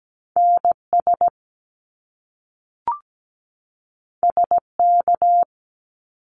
Morse code, heard as a keyed tone, about 17 wpm, 700 Hz. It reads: NS SK
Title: Canadian Provinces and Territories 17wpm